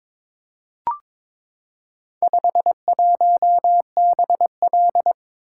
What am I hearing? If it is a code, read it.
Morse code, heard as a keyed tone, about 22 wpm, 700 Hz. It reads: 51BL
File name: practice / 22wpm / Four Letter-Number Combinations - 22wpm